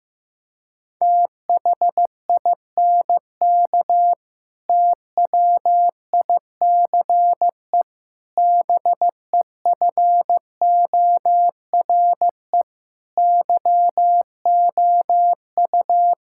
Morse code, heard as a keyed tone, about 15 wpm, 700 Hz. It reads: THINK TWICE BEFORE YOU
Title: Sentences from Top 1600 Words 15wpm